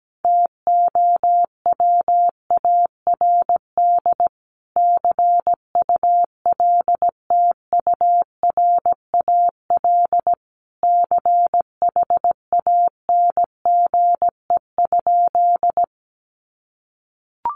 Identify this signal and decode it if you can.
Morse code, heard as a keyed tone, about 17 wpm, 700 Hz. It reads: TOWARD CULTURAL CHANGE?
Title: Sentences from Top 1500 Words 17wpm